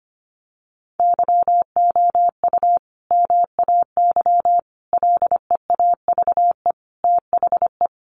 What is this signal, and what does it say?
Morse code, heard as a keyed tone, about 25 wpm, 700 Hz. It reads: YOU MAY LEAVE THE
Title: Sentences from Top 400 Words 25wpm